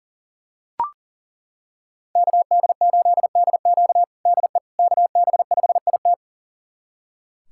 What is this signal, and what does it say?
Morse code, heard as a keyed tone, about 40 wpm, 700 Hz. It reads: KD8DQ DE KB5IT
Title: USA Callsign DE Callsign - 5 Characters - 40wpm